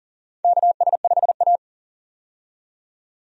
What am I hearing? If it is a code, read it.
Morse code, heard as a keyed tone, about 40 wpm, 700 Hz. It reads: KS5A
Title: USA Call Signs - 4 Characters - 40wpm